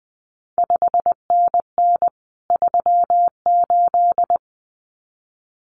Morse code, heard as a keyed tone, about 20 wpm, 700 Hz. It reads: 5NN 38
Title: CQ WW DX Contest Exchange Encore - 20wpm